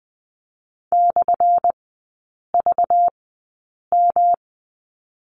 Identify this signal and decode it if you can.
Morse code, heard as a keyed tone, about 20 wpm, spaced as for 8 wpm, 700 Hz. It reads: /VM
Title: 191 Lesson 31e - Letter v - Three Characters 20wpm